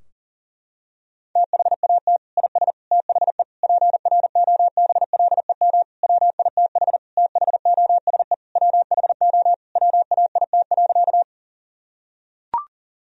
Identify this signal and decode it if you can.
Morse code, heard as a keyed tone, about 40 wpm, 700 Hz. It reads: THAT IS THE PROBLEM WITH THOSE WHO WAIT.
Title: Sentences from Top 1000 Words - Review Part 2 - 40wpm